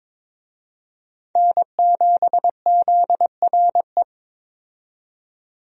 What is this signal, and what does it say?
Morse code, heard as a keyed tone, about 22 wpm, 700 Hz. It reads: N7ZRE
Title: USA Call Signs - 5 Characters - 22wpm